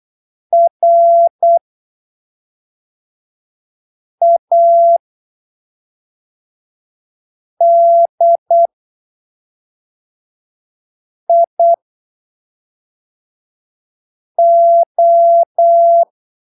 Morse code, heard as a keyed tone, about 8 wpm, spaced as for 3 wpm, 680 Hz. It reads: RADIO
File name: arquivos / texto / texto 14 08ppm